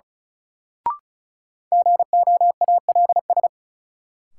Morse code, heard as a keyed tone, about 34 wpm, 700 Hz. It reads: GOALS
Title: Sentences from Top 1000 Words - Review Part 6 - New Word Drill - 35wpm